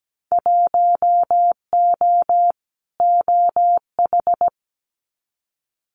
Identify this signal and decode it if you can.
Morse code, heard as a keyed tone, about 17 wpm, 700 Hz. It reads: 1O OH
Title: Winter Field Day Exchanges 17wpm